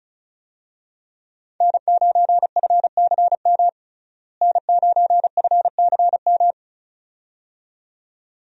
Morse code, heard as a keyed tone, about 34 wpm, 700 Hz. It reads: N9FCM N9FCM
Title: USA Call Signs - 5 Characters - Repeated 2x - 35wpm